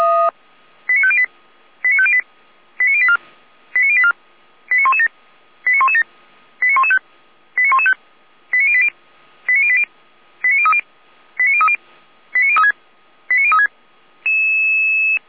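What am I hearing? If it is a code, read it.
Five-tone selective call (ZVEI), heard as quick runs of five tones, tones of about 70 ms. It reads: 89508 89508 89084 89084 89117 89117 89115 89115 89080 89080 89030 89030 89036 89036
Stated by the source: Received on an SDR; ZVEI five-tone selective call.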